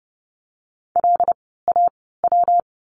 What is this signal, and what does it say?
Morse code, heard as a keyed tone, about 30 wpm, spaced as for 17 wpm, 700 Hz. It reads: LAW
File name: course / 30:10wpm / 115 Lesson 19f - Letter w - Words 30wpm